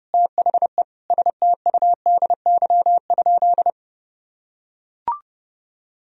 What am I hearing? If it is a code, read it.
Morse code, heard as a keyed tone, about 30 wpm, 700 Hz. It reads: THE STUDY?
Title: Sentences from Top 1000 Words - Review Part 2 - 30wpm